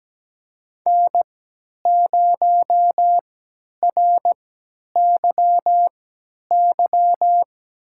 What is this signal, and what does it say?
Morse code, heard as a keyed tone, about 17 wpm, spaced as for 10 wpm, 700 Hz. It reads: N0RYY
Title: USA Call Signs - 3x Character Spacing - 17wpm